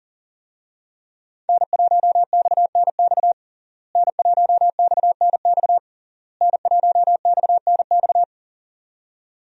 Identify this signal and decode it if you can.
Morse code, heard as a keyed tone, about 40 wpm, 700 Hz. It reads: N1XNX N1XNX N1XNX
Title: USA Call Signs - 5 Characters - Repeated 3x - 40wpm